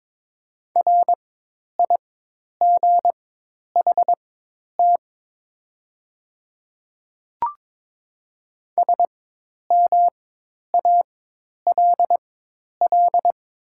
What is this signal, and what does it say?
Morse code, heard as a keyed tone, about 22 wpm, spaced as for 10 wpm, 700 Hz. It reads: RIGHT SMALL
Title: 4x Character Spacing - Top 200 Words - 22wpm